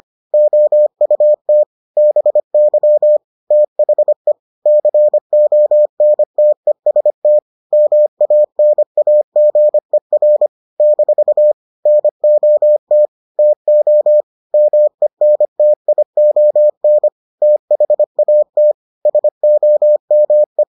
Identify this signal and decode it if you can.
Morse code, heard as a keyed tone, about 25 wpm, 600 Hz. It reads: OUT BY THE CONTEST MANAGER - NOT TO MENTION THAT SOME